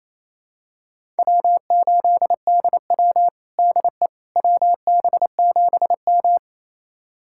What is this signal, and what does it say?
Morse code, heard as a keyed tone, about 28 wpm, 700 Hz. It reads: W8DW DE WB7M
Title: USA Callsign DE Callsign - 4 Characters - 28wpm